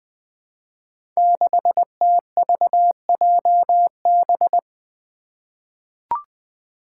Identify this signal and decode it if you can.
Morse code, heard as a keyed tone, about 20 wpm, 700 Hz. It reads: 6TVJB
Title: Five Challenging Letter-Number Combinations 20wpm